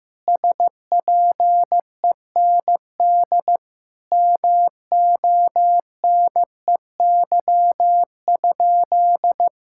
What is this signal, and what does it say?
Morse code, heard as a keyed tone, about 15 wpm, 700 Hz. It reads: SPEND MONEY?
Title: Sentences from Top 1000 Words - Review Part 5 - 15wpm